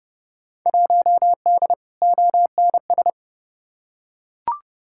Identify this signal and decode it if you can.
Morse code, heard as a keyed tone, about 30 wpm, 700 Hz. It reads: 1D ONS
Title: Field Day Exchanges - 30wpm